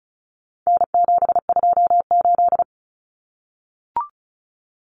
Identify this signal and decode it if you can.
Morse code, heard as a keyed tone, about 34 wpm, 700 Hz. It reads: N728
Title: Common Cut Numbers - 4-Digit 35wpm